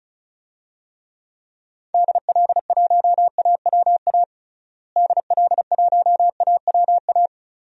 Morse code, heard as a keyed tone, about 36 wpm, 700 Hz. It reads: DL1AWA DL1AWA
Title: DXCC-100 Call Signs - 6 Characters - Repeated 3x - 35wpm